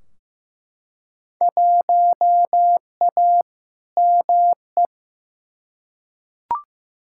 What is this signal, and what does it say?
Morse code, heard as a keyed tone, about 15 wpm, 700 Hz. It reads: 1A ME
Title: Field Day Exchanges Encore - 15wpm